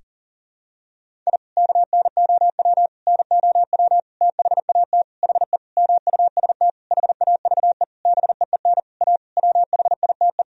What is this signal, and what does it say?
Morse code, heard as a keyed tone, about 40 wpm, 700 Hz. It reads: I KNOW NOW THAT HE MUST HAVE BEEN A WHITE
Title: Sentences from Top 1000 Words - Review Part 1 - 40wpm